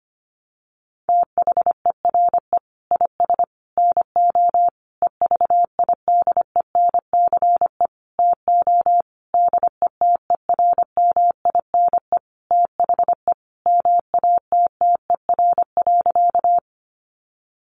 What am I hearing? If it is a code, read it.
Morse code, heard as a keyed tone, about 25 wpm, 700 Hz. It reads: THERE IS NO EVIDENCE TO DETERMINE THE MATTER.
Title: Sentences from Top 1300 Words 25wpm